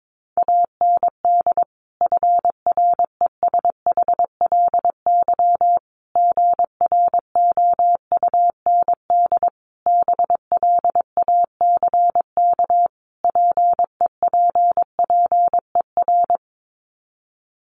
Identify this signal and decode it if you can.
Morse code, heard as a keyed tone, about 22 wpm, 700 Hz. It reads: AND FRESHLY GROUND BLACK PEPPER
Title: Sets of 5 Words Encore 22wpm